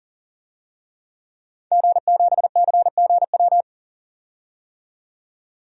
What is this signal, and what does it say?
Morse code, heard as a keyed tone, about 40 wpm, 700 Hz. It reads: G7CGW
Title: DXCC-100 Call Signs - 5 Characters - 40wpm